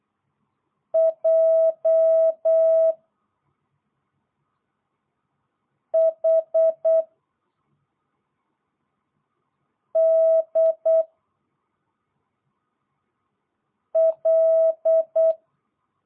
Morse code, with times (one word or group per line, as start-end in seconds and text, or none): 0.9-2.9 J
5.9-7.0 H
10.0-11.0 D
14.0-15.3 L